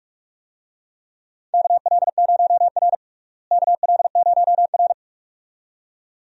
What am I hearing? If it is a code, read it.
Morse code, heard as a keyed tone, about 44 wpm, 700 Hz. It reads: KL0R KL0R
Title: USA Call Signs - 4 Characters - Repeated 2x - 45wpm